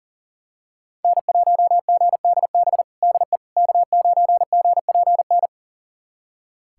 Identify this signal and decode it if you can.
Morse code, heard as a keyed tone, about 40 wpm, 700 Hz. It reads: N1GDB DE K9GPN
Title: USA Callsign DE Callsign - 5 Characters - 40wpm